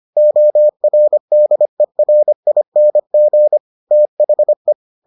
Morse code, heard as a keyed tone, about 25 wpm, 600 Hz. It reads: ORDERING THE